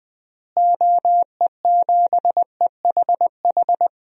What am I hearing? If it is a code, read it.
Morse code, heard as a keyed tone, about 20 wpm, 700 Hz. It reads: OE7EHH
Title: DXCC-100 Call Signs - 6 Characters - Repeated 3x - 20wpm